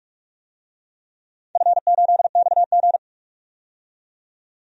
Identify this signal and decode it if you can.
Morse code, heard as a keyed tone, about 47 wpm, 700 Hz. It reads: F8XG
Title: DXCC-100 Call Signs - 4 Characters - 45wpm